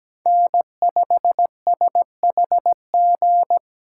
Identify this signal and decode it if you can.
Morse code, heard as a keyed tone, about 17 wpm, 700 Hz. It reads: N5SHG
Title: USA Call Signs - 5 Characters - 2x Character Spacing - 17wpm